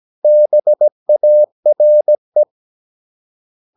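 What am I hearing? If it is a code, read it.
Morse code, heard as a keyed tone, about 17 wpm, 600 Hz. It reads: BARE
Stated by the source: Slow (QRS) Morse practice at 17 wpm, from CW med Gnister.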